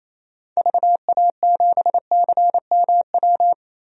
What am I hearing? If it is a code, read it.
Morse code, heard as a keyed tone, about 28 wpm, 700 Hz. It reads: VA7CMW
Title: DXCC-100 Call Signs - 6 Characters - Repeated 3x - 28wpm